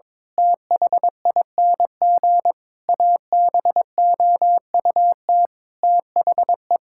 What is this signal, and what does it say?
Morse code, heard as a keyed tone, about 22 wpm, 700 Hz. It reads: THING ABOUT THE
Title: Sentences from Top 1000 Words - Review Part 5 - 22wpm